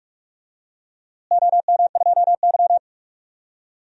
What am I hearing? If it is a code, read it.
Morse code, heard as a keyed tone, about 46 wpm, 700 Hz. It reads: OM2Y
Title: DXCC-100 Call Signs - 4 Characters - Repeated 3x - 45wpm